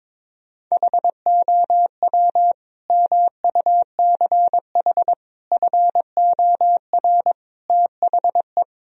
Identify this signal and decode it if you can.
Morse code, heard as a keyed tone, about 22 wpm, 700 Hz. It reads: HOW MUCH FOR THE
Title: Sentences from Top 1000 Words - Review Part 3 - 22wpm